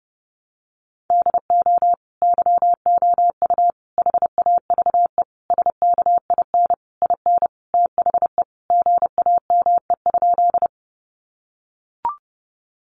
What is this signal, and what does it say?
Morse code, heard as a keyed tone, about 30 wpm, 700 Hz. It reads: DO YOU HAVE SKIN IN THE GAME?
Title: Sentences from Top 1300 Words 30wpm